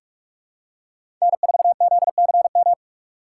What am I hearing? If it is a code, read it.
Morse code, heard as a keyed tone, about 45 wpm, 700 Hz. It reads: N4ZCM
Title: USA Call Signs - 5 Characters - 2x Character Spacing - 45wpm